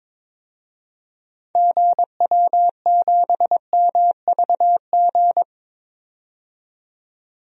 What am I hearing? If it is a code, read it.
Morse code, heard as a keyed tone, about 22 wpm, 700 Hz. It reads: GW7MVG
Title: DXCC-100 Call Signs - 6 Characters - 22wpm